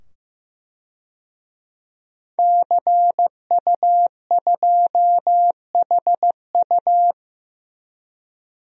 Morse code, heard as a keyed tone, about 15 wpm, 700 Hz. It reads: CU2HU
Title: DXCC-100 Call Signs - 5 Characters - 15wpm